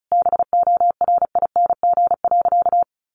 Morse code, heard as a keyed tone, about 34 wpm, 700 Hz. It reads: BORING.
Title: Easy Phrases 35wpm